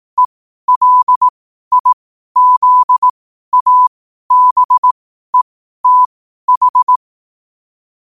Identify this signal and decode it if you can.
Morse code, heard as a keyed tone, about 18 wpm, spaced as for 12 wpm, 1000 Hz. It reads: ELIZABETH
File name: qotdfiles / 12wpm